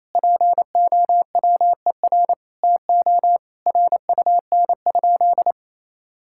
Morse code, heard as a keyed tone, about 28 wpm, 700 Hz. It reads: POWER TO RUN?
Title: Sentences from Top 1000 Words - Review Part 6 - 28wpm